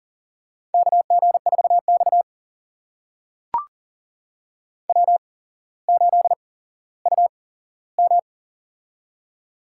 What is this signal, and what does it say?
Morse code, heard as a keyed tone, about 40 wpm, spaced as for 17 wpm, 700 Hz. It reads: KG4X W8UM
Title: USA Call Signs - 4 Characters - 8x Character Spacing - 40wpm